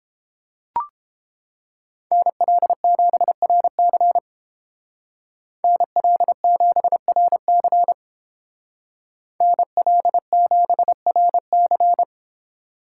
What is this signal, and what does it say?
Morse code, heard as a keyed tone, about 28 wpm, 700 Hz. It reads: NL7RC NL7RC NL7RC